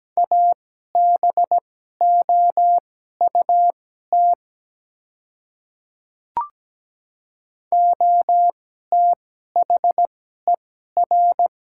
Morse code, heard as a keyed tone, about 17 wpm, spaced as for 12 wpm, 700 Hz. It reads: ABOUT OTHER
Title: 2x Character Spacing - Top 100 Words - 17wpm